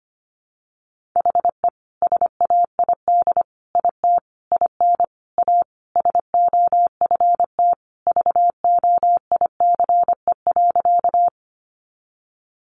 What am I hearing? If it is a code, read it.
Morse code, heard as a keyed tone, about 25 wpm, 700 Hz. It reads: HE SAID IT IN A SOFT VOICE.
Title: Sentences from Top 1500 Words 25wpm